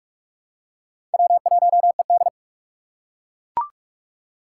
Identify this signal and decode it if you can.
Morse code, heard as a keyed tone, about 42 wpm, 700 Hz. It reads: W1ED
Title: USA Call Signs - 4 Characters - 2x Character Spacing - 45wpm